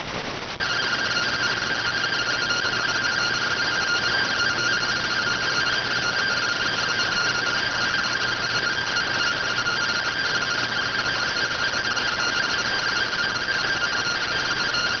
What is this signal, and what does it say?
Two-tone teleprinter signal, about 170 Hz shift.